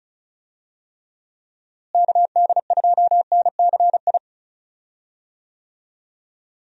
Morse code, heard as a keyed tone, about 34 wpm, 700 Hz. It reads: KD2NCI
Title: USA Call Signs - 6 Characters - Repeated 2x - 35wpm